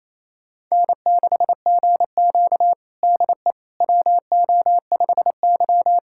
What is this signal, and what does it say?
Morse code, heard as a keyed tone, about 28 wpm, 700 Hz. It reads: N6GQ DE WO5Y
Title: USA Callsign DE Callsign - 4 Characters - 28wpm